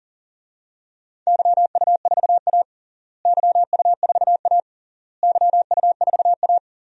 Morse code, heard as a keyed tone, about 37 wpm, 700 Hz. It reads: YU4A YU4A YU4A
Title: DXCC-100 Call Signs - 4 Characters - Repeated 3x - 40wpm